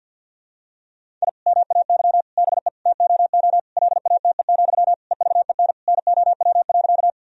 Morse code, heard as a keyed tone, about 46 wpm, 700 Hz. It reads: I MAY BE TOO LATE, EVEN NOW.
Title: Sentences from Top 1000 Words - Review Part 1 - 50wpm